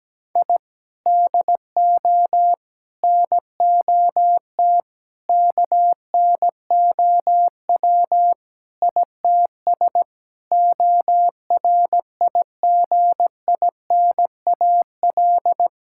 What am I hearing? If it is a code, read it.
Morse code, heard as a keyed tone, about 17 wpm, 700 Hz. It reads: I DO NOT KNOW ITS ORIGINAL
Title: Sentences from Top 1000 Words - Review Part 6 - 17wpm